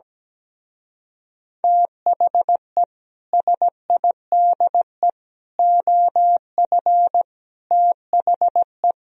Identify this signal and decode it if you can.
Morse code, heard as a keyed tone, about 17 wpm, 700 Hz. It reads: THE SIDE OF THE
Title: Speed-Racing - Sets of 5 Words 17wpm